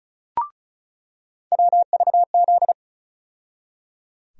Morse code, heard as a keyed tone, about 36 wpm, 700 Hz. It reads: WVZ
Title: USA Call Sign Postfixes - 3 Letters 35wpm